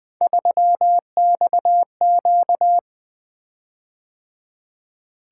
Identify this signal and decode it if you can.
Morse code, heard as a keyed tone, about 20 wpm, 700 Hz. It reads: 3XQ
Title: Three Letter-Number Combinations - 20wpm